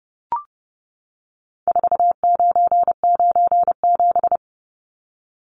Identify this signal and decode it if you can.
Morse code, heard as a keyed tone, about 30 wpm, 700 Hz. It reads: 4997
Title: Numbers - 4-Digit 30wpm